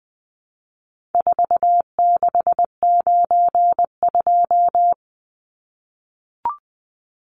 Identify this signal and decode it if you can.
Morse code, heard as a keyed tone, about 20 wpm, 700 Hz. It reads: 4692
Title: Numbers - 4-Digit 20wpm